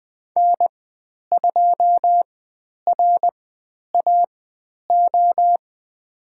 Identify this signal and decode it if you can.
Morse code, heard as a keyed tone, about 20 wpm, spaced as for 10 wpm, 700 Hz. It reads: N2RAO
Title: 095 Lesson 16g - Letter c - Call Signs 20wpm